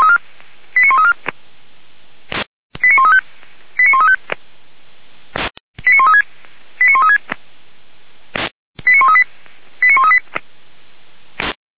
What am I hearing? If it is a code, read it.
Five-tone selective call (ZVEI), heard as quick runs of five tones, tones of about 80 ms. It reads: 89134 89135 89135 89136 89136 89138 89138
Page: /464/SDR/ZVEI/